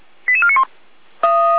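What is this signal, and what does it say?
Five-tone selective call (ZVEI), heard as quick runs of five tones, tones of about 70 ms. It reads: 88491
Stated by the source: Five-tone ZVEI selective call received on an SDR.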